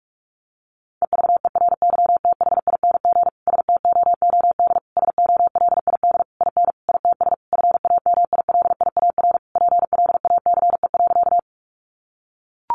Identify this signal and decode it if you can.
Morse code, heard as a keyed tone, about 44 wpm, 700 Hz. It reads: EVERYTHING STOOD SOLID IN ITS FAMILIAR PLACE.
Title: Sentences from Top 1500 Words 45wpm